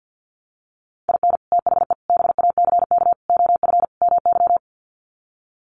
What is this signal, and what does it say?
Morse code, heard as a keyed tone, about 50 wpm, 700 Hz. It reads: IN THE BACK OF MY
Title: Sets of 5 Words Encore 50wpm